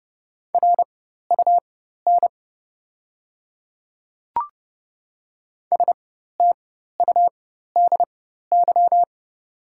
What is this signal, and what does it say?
Morse code, heard as a keyed tone, about 30 wpm, spaced as for 14 wpm, 700 Hz. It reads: RUN STUDY
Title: 4x Character Spacing - Top 300 Words - 30wpm